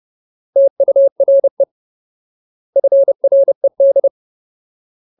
Morse code, heard as a keyed tone, about 30 wpm, 550 Hz. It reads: TURE FRED
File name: Ultimate Word List / 30WPM Ultimate